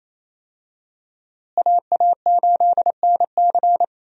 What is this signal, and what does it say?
Morse code, heard as a keyed tone, about 28 wpm, 700 Hz. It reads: AA8NC
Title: USA Call Signs - 5 Characters - Repeated 3x - 28wpm